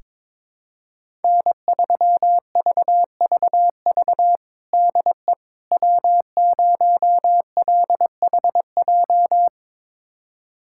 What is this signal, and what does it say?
Morse code, heard as a keyed tone, about 21 wpm, 700 Hz. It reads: N3VVV DE W0LHJ
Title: USA Callsign DE Callsign - 5 Characters - 22wpm